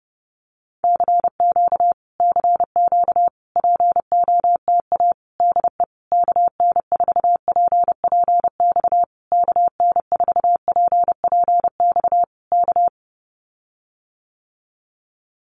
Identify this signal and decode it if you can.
Morse code, heard as a keyed tone, about 30 wpm, 700 Hz. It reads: CQ CQ POTA DE KN4PPX KN4PPX K
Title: Common QSO Phrases Encore 30wpm